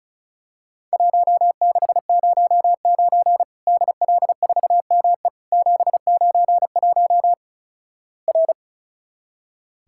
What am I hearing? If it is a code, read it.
Morse code, heard as a keyed tone, about 34 wpm, 700 Hz. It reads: 1609 DL4ME 791 R
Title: Worked All Europe DX Contest - QTC Exchange - 35wpm